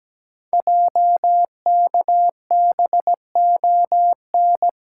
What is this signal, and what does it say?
Morse code, heard as a keyed tone, about 17 wpm, 700 Hz. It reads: JKBON
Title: Code Groups - 5 Letter:Numbers - 17wpm